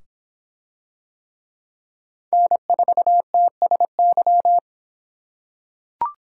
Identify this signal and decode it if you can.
Morse code, heard as a keyed tone, about 26 wpm, 700 Hz. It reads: N4TSY